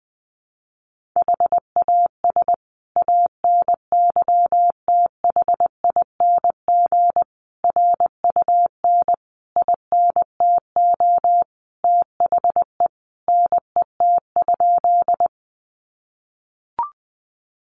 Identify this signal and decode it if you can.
Morse code, heard as a keyed tone, about 20 wpm, 700 Hz. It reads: HAS ANYTHING RUN INTO THE NET?
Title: Sentences from Top 1400 Words 20wpm